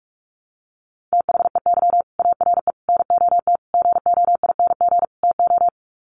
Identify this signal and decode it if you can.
Morse code, heard as a keyed tone, about 44 wpm, 700 Hz. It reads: THEY ARE NOT GOING TO
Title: Sets of 5 Words Encore2 45wpm